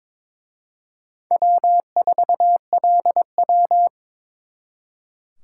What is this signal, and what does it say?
Morse code, heard as a keyed tone, about 26 wpm, 700 Hz. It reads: W4LW
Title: Speed Racing - USA 1x2 Call Signs 22wpm